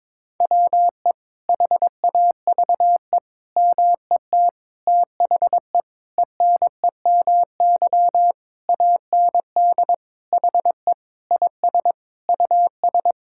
Morse code, heard as a keyed tone, about 22 wpm, 700 Hz. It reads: WE HAVE MET THE ENEMY AND HE IS US
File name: Sets of 9 Words - 22wpm